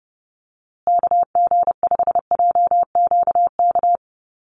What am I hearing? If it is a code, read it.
Morse code, heard as a keyed tone, about 30 wpm, 700 Hz. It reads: KG5JQK
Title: USA Call Signs Repeated 2x 30wpm